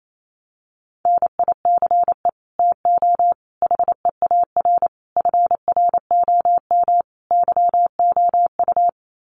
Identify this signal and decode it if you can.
Morse code, heard as a keyed tone, about 28 wpm, 700 Hz. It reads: NICE TO HEAR FROM YOU
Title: Common QSO Phrases 28wpm